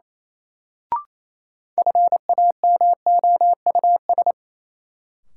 Morse code, heard as a keyed tone, about 28 wpm, 700 Hz. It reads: FAMOUS
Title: Sentences from Top 1500 Words - New Word Drill Alt - 28wpm